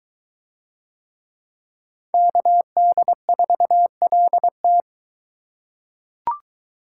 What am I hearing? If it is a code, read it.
Morse code, heard as a keyed tone, about 23 wpm, 700 Hz. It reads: KD4LT